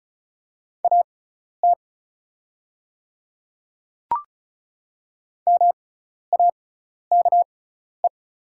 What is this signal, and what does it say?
Morse code, heard as a keyed tone, about 34 wpm, spaced as for 12 wpm, 700 Hz. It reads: AT MAKE
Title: 6x Character Spacing - Top 100 Words - 35wpm